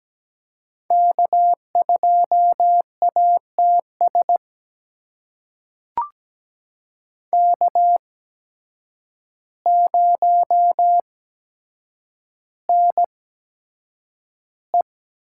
Morse code, heard as a keyed tone, about 17 wpm, spaced as for 8 wpm, 700 Hz. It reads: K2ATS K0NE
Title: USA Call Signs - 8x Character Spacing - 17wpm